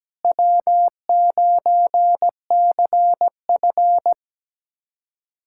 Morse code, heard as a keyed tone, about 17 wpm, 700 Hz. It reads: W9CF
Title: USA Callsign DE Callsign - 4 Characters - 17wpm